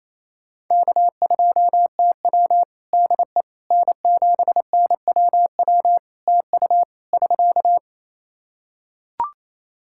Morse code, heard as a keyed tone, about 28 wpm, 700 Hz. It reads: K2TW DE N7NWW TU <SK>
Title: K1USN SST QSO Practice 28wpm